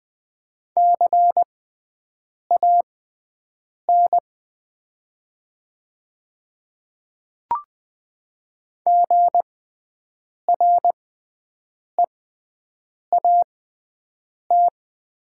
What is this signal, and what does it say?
Morse code, heard as a keyed tone, about 20 wpm, spaced as for 7 wpm, 700 Hz. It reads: CAN GREAT
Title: 6x Character Spacing - Top 200 Words - 20wpm